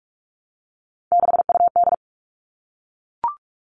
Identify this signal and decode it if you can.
Morse code, heard as a keyed tone, about 44 wpm, 700 Hz. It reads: 6UD
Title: All Cut Numbers - 3-Digit 45wpm